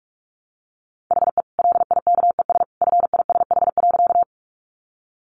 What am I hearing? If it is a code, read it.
Morse code, heard as a keyed tone, about 44 wpm, 700 Hz. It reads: HE LIKES FISH.
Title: Additional Easy Phrases 45wpm